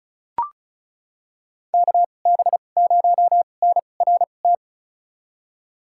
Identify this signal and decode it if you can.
Morse code, heard as a keyed tone, about 34 wpm, spaced as for 25 wpm, 700 Hz. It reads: KB0NRT
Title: USA Call Signs - 6 Characters - 2x Character Spacing - 35wpm